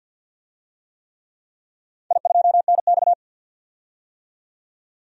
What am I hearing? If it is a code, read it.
Morse code, heard as a keyed tone, about 47 wpm, 700 Hz. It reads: I2NX
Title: DXCC-100 Call Signs - 4 Characters - 50wpm